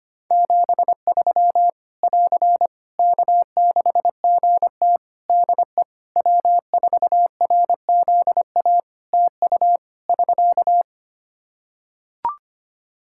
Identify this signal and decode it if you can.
Morse code, heard as a keyed tone, about 25 wpm, 700 Hz. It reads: 73 <AR> K6GT DE W4RZA TU <SK>
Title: Practice QSO - Exchange 4 - Ending - 25wpm